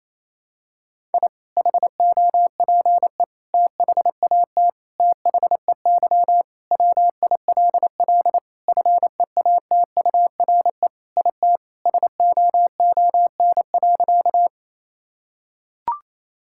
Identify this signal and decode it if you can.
Morse code, heard as a keyed tone, about 28 wpm, 700 Hz. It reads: I HOPE THAT THEY WILL FEATURE IT SOON.
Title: Sentences from Top 1000 Words - Review Part 5 - 28wpm